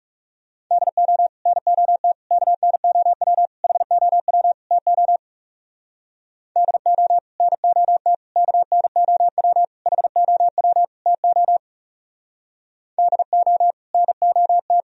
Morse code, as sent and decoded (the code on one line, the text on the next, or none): -.. --- / -. --- - / -.- -. --- .-- / .... --- .-- / - --- / -.. --- / -. --- - / -.- -. --- .-- / .... --- .-- / - --- / -.. --- / -. --- -
DO NOT KNOW HOW TO DO NOT KNOW HOW TO DO NOT